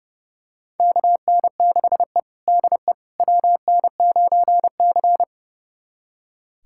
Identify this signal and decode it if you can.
Morse code, heard as a keyed tone, about 30 wpm, 700 Hz. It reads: KN6E DE WN9C
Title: USA Callsign DE Callsign - 4 Characters - 30wpm